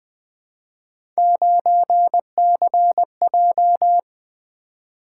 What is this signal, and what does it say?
Morse code, heard as a keyed tone, about 20 wpm, 700 Hz. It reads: 9CJ
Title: Three Letter-Number Combinations - 20wpm